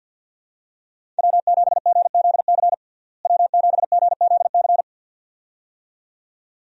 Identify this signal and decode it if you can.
Morse code, heard as a keyed tone, about 50 wpm, 700 Hz. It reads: W7GZC W7GZC
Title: USA Call Signs - 5 Characters - Repeated 2x - 50wpm